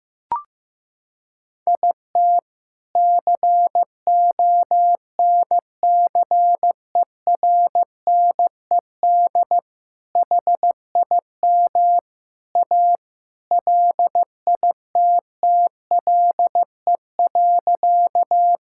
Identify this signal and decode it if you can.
Morse code, heard as a keyed tone, about 15 wpm, 700 Hz. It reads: IT CONCERNED HIM A LITTLE.
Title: Sentences from Top 1000 Words - Review Part 6 - 15wpm